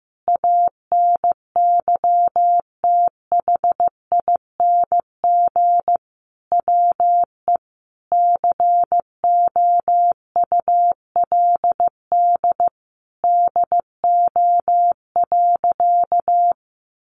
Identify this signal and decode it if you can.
Morse code, heard as a keyed tone, about 15 wpm, 700 Hz. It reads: ANYTHING WE COULD DO.
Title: Sentences from Top 1400 Words 15wpm